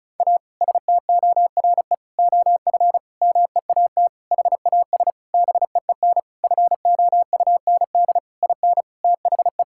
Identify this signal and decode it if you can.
Morse code, heard as a keyed tone, about 36 wpm, 700 Hz. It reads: A STORE OF MEAT HAS BEEN FOUND IN THE
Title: Sentences from Top 1000 Words - Review Part 5 - 35wpm